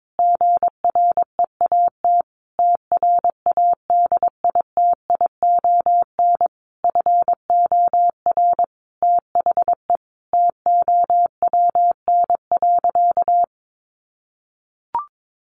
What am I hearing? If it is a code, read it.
Morse code, heard as a keyed tone, about 22 wpm, 700 Hz. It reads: GREAT TRADITION FOR THE TOWN.
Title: Sentences from Top 1500 Words 22wpm